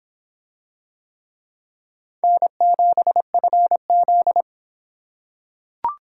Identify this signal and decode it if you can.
Morse code, heard as a keyed tone, about 26 wpm, 700 Hz. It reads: N7FZ